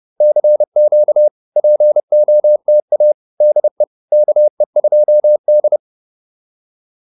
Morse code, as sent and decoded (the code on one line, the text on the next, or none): -.-. --.- / .--. --- - .- / -.. / -.- . ..--- -..
CQ POTA DE KE2D